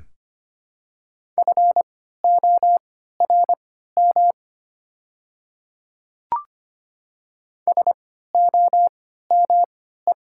Morse code, heard as a keyed tone, about 25 wpm, spaced as for 14 wpm, 690 Hz. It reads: FORM SOME
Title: 3x Character Spacing - Top 300 Words - 25wpm